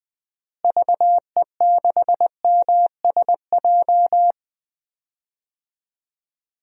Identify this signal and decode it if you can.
Morse code, heard as a keyed tone, about 20 wpm, 700 Hz. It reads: VE6MSJ
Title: DXCC-100 Call Signs - 6 Characters - Repeated 3x - 20wpm